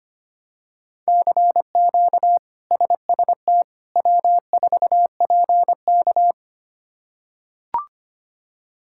Morse code, heard as a keyed tone, about 25 wpm, 700 Hz. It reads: CQ SST W4PK
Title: K1USN SST QSO Practice - 25wpm